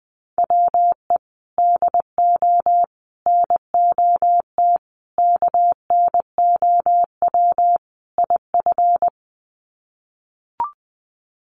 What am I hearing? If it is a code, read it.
Morse code, heard as a keyed tone, about 20 wpm, 700 Hz. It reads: WE DO NOT KNOW IF
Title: Sets of 5 Words Encore2 20wpm